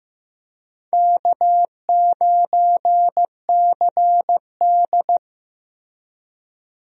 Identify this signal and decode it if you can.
Morse code, heard as a keyed tone, about 15 wpm, 700 Hz. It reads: K9CD
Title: USA Call Signs - 4 Characters - Repeated 2x - 15wpm